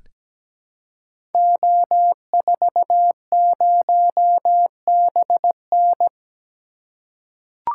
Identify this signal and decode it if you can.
Morse code, heard as a keyed tone, about 17 wpm, 700 Hz. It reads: O40BN
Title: Code Groups - 5 Letter:Numbers - 17wpm